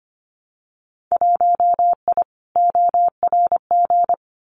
Morse code, heard as a keyed tone, about 25 wpm, 700 Hz. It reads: 1I ORG
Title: Winter Field Day Exchanges 25wpm